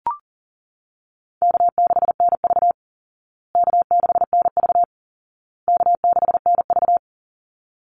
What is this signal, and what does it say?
Morse code, heard as a keyed tone, about 40 wpm, 700 Hz. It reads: K6NV K6NV K6NV
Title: Easy Call Signs Repeated 3x - 40wpm